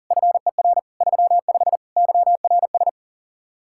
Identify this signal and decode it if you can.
Morse code, heard as a keyed tone, about 51 wpm, 700 Hz. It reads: FER 35 YRS